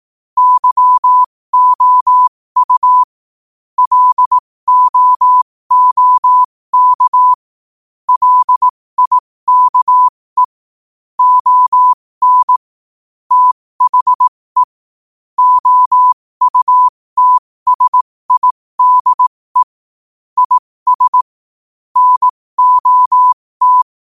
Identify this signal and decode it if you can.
Morse code, heard as a keyed tone, about 18 wpm, spaced as for 15 wpm, 1000 Hz. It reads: YOU LOOK LIKE ON THE OUTSIDE IS NOT
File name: qotdfiles / 15wpm